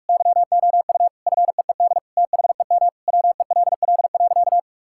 Morse code, heard as a keyed tone, about 46 wpm, 700 Hz. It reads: YOU FEED THEM WELL.
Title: Sentences from Top 1600 Words 45wpm